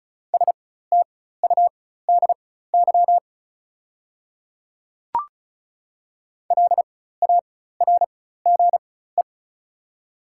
Morse code, heard as a keyed tone, about 36 wpm, spaced as for 16 wpm, 690 Hz. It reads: STUDY LARGE
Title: 4x Character Spacing - Top 500 Words - 35wpm